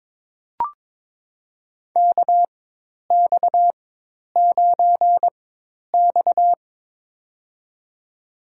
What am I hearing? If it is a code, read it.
Morse code, heard as a keyed tone, about 22 wpm, spaced as for 10 wpm, 700 Hz. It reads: KX9X
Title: USA Call Signs - 4x Character Spacing - 22wpm